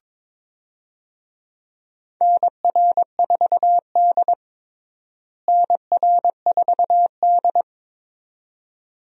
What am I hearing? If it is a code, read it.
Morse code, heard as a keyed tone, about 22 wpm, 700 Hz. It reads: NR4D NR4D
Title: USA Call Signs - 4 Characters - Repeated 2x - 22wpm